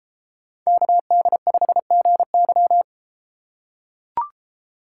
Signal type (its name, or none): Morse code